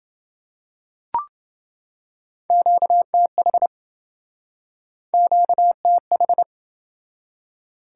Morse code, heard as a keyed tone, about 27 wpm, 700 Hz. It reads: QTH QTH